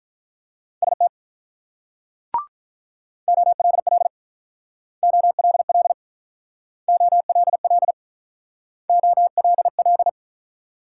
Morse code, heard as a keyed tone, about 47 wpm, 690 Hz. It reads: ST OLL OLL OLL OLL